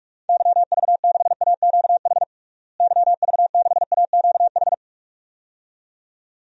Morse code, heard as a keyed tone, about 47 wpm, 700 Hz. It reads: YV6AQH YV6AQH
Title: DXCC-100 Call Signs - 6 Characters - Repeated 2x - 45wpm